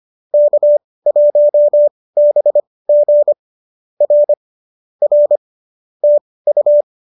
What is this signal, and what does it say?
Morse code, heard as a keyed tone, about 25 wpm, spaced as for 18 wpm, 570 Hz. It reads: K1BG R R TU